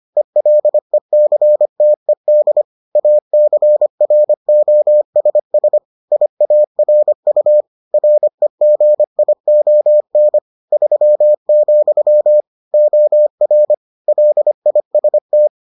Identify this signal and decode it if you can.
Morse code, heard as a keyed tone, about 25 wpm, 600 Hz. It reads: ELECTED ACROSS IARU REGION 3, OR LIST